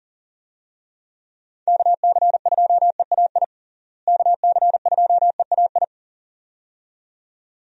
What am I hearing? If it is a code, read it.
Morse code, heard as a keyed tone, about 40 wpm, 700 Hz. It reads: KC2EAI KC2EAI
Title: USA Call Signs - 6 Characters - Repeated 2x - 40wpm